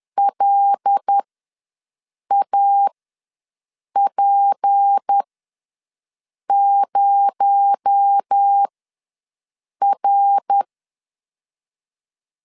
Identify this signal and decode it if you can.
Morse code, heard as a keyed tone, about 10 wpm, spaced as for 6 wpm, 800 Hz. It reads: LAP0R
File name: LEZIONI / 6 PKX0 / 6 PKX0 ws 10wpm 4220